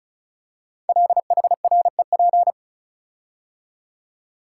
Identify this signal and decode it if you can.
Morse code, heard as a keyed tone, about 36 wpm, 700 Hz. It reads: LHREP
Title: Code Groups - 5 Letters - 35wpm